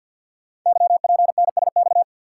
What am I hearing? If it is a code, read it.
Morse code, heard as a keyed tone, about 50 wpm, 700 Hz. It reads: YPNSX
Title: Code Groups - 5 Letter:Numbers - 50wpm